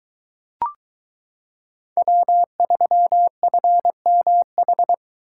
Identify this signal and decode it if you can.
Morse code, heard as a keyed tone, about 22 wpm, 700 Hz. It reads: W3FMH